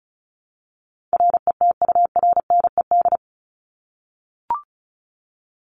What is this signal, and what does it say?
Morse code, heard as a keyed tone, about 34 wpm, 700 Hz. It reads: RETURNED
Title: ED Suffix - Top 100 Words with Base Words 35wpm